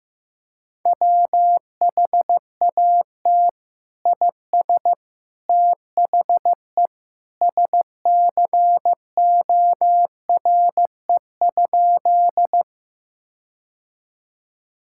Morse code, heard as a keyed tone, about 15 wpm, 700 Hz. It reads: WHAT IS THE SCORE?
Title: Sentences from Top 1600 Words 15wpm